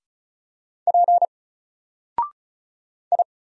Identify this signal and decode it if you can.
Morse code, heard as a keyed tone, about 33 wpm, 700 Hz. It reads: P I
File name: Single-Letter-Number-35wpm